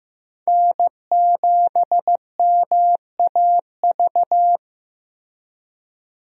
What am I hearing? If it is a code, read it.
Morse code, heard as a keyed tone, about 15 wpm, 700 Hz. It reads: N7MAV
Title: USA Call Signs - 5 Characters - Repeated 3x - 15wpm